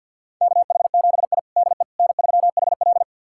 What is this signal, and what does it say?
Morse code, heard as a keyed tone, about 50 wpm, 700 Hz. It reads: KH7I DE N3HL
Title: USA Callsign DE Callsign - 4 Characters - 50wpm